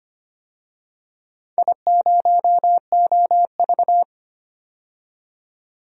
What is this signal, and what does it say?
Morse code, heard as a keyed tone, about 25 wpm, 700 Hz. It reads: I0OV
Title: DXCC-100 Call Signs - 4 Characters - 25wpm